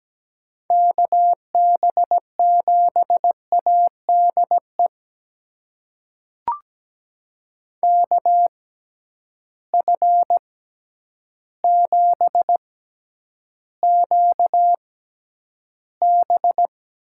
Morse code, heard as a keyed tone, about 17 wpm, spaced as for 9 wpm, 700 Hz. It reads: KB7ADE KF7QB
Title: USA Call Signs - 6 Characters - 6x Character Spacing - 17wpm